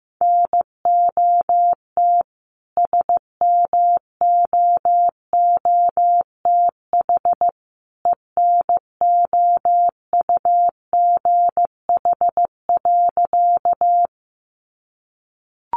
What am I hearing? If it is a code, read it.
Morse code, heard as a keyed tone, about 15 wpm, 700 Hz. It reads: NOT SMOOTH ENOUGH.
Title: Sentences from Top 1400 Words 15wpm